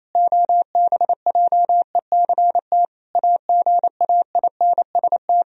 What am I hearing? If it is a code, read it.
Morse code, heard as a keyed tone, about 28 wpm, 700 Hz. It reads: OBJECT AGAINST
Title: Sentences from Top 1000 Words - Review Part 3 - 28wpm